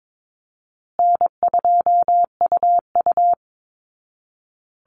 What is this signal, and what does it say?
Morse code, heard as a keyed tone, about 22 wpm, 700 Hz. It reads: N2UU
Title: Call Signs Repeated 2x - 22wpm